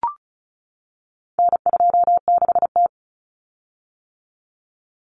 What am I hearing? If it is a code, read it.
Morse code, heard as a keyed tone, about 34 wpm, 700 Hz. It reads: N26T
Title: Common Cut Numbers - 4-Digit 35wpm